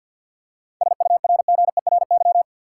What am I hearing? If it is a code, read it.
Morse code, heard as a keyed tone, about 46 wpm, 700 Hz. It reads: SURGERY
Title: Seven Letter Words - 50wpm